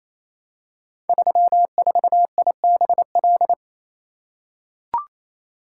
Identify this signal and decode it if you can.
Morse code, heard as a keyed tone, about 29 wpm, 700 Hz. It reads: 34IBL
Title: Code Groups - 5 Letter:Numbers - 28wpm